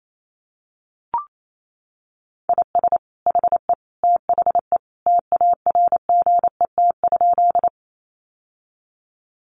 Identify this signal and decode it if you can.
Morse code, heard as a keyed tone, about 28 wpm, 700 Hz. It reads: IS HE THE TARGET?
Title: Sentences from Top 1200 Words 28wpm